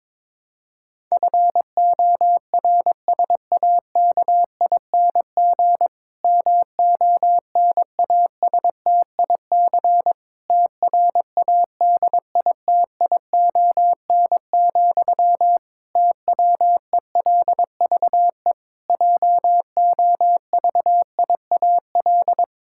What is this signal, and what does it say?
Morse code, heard as a keyed tone, about 22 wpm, 700 Hz. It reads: FORSAKING MONASTIC TRADITION, TWELVE JOVIAL
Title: Pangram Challenge v2 - 22wpm